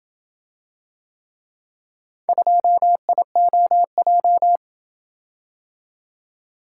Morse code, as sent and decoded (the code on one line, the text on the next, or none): ..--- .. --- .---
2IOJ